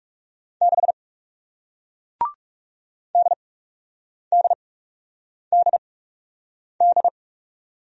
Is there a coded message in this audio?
Yes: Morse code from 0.6 to 7.1 s, heard as a keyed tone, about 41 wpm, spaced as for 8 wpm, 700 Hz. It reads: 6 DDDD